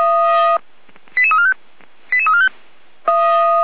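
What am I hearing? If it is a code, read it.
Five-tone selective call (ZVEI), heard as quick runs of five tones, tones of about 70 ms. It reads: 88345 88345
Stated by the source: ZVEI five-tone selective call received on an SDR.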